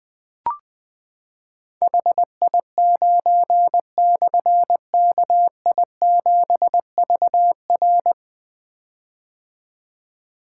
Morse code, heard as a keyed tone, about 20 wpm, 700 Hz. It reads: HI9/KI7VR
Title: Difficult Call Signs 20wpm